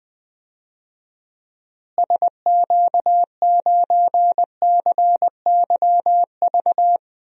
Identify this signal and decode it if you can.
Morse code, heard as a keyed tone, about 20 wpm, 700 Hz. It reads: SQ9CYV
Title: DXCC-100 Call Signs - 6 Characters - Repeated 3x - 20wpm